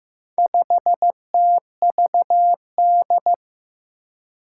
Morse code, heard as a keyed tone, about 15 wpm, 700 Hz. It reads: 5TVD
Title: Four Challenging Letter-Number Combinations 15wpm